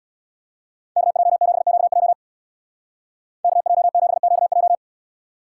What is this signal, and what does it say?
Morse code, heard as a keyed tone, about 62 wpm, 700 Hz. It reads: KJ7QJ KJ7QJ
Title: Speed Racing - USA Call Signs 50wpm